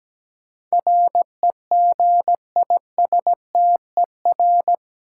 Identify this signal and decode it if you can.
Morse code, heard as a keyed tone, about 17 wpm, 700 Hz. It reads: REGISTER
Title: Sentences from Top 1200 Words - New Word Drill - 17wpm